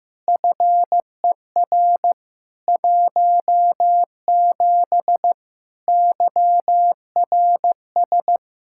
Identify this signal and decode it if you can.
Morse code, heard as a keyed tone, about 15 wpm, 700 Hz. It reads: FER 17 YRS
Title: Practice QSO - Exchange 3 - More Info - 15wpm